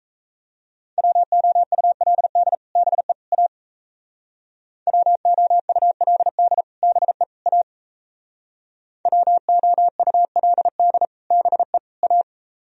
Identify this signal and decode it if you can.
Morse code, heard as a keyed tone, about 35 wpm, 700 Hz. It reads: WOULD BE A WOULD BE A WOULD BE A
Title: Speed-Racing - Sets of 3 Words 28wpm